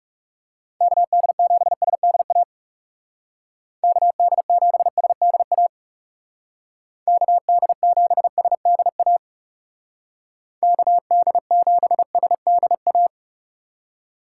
Morse code, heard as a keyed tone, about 37 wpm, 700 Hz. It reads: KD7SDA KD7SDA KD7SDA KD7SDA
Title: Speed Racing - USA 2x3 Call Signs 30wpm